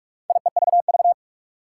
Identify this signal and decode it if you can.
Morse code, heard as a keyed tone, about 46 wpm, 700 Hz. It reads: IEVV
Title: Four Challenging Letter-Number Combinations 45wpm